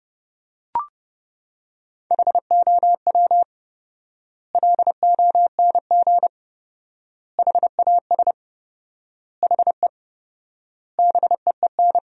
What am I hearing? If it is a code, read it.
Morse code, heard as a keyed tone, about 32 wpm, 700 Hz. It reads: HOW LONG HAS HE BEEN
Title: 4x Word-Spacing - Sentences from Top 100 Words - 30wpm